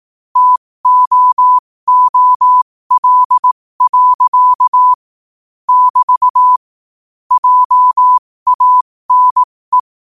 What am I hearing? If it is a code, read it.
Morse code, heard as a keyed tone, about 18 wpm, spaced as for 15 wpm, 1000 Hz. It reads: TOOL. <BT> JANE
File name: qotdfiles / 15wpm